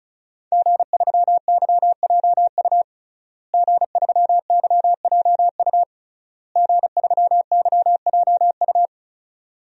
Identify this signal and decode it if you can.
Morse code, heard as a keyed tone, about 36 wpm, 700 Hz. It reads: G3YJU G3YJU G3YJU
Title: DXCC-100 Call Signs - 5 Characters - Repeated 3x - 35wpm